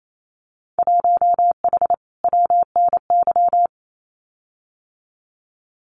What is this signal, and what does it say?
Morse code, heard as a keyed tone, about 28 wpm, 700 Hz. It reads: 1H WNY
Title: Winter Field Day Exchanges 28wpm